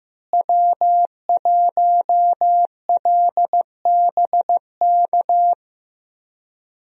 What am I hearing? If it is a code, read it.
Morse code, heard as a keyed tone, about 16 wpm, 700 Hz. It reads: W1LBK
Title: USA Call Signs - 5 Characters - Repeated 3x - 15wpm